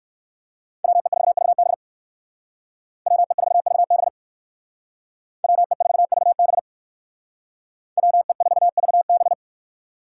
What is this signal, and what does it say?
Morse code, heard as a keyed tone, about 55 wpm, 700 Hz. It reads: WE4VB WE4VB WE4VB WE4VB
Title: Speed Racing - USA 2x2 Call Signs 45wpm